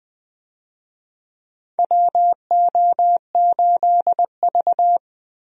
Morse code, heard as a keyed tone, about 20 wpm, 700 Hz. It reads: WO8V
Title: USA Call Signs - 4 Characters - Repeated 2x - 20wpm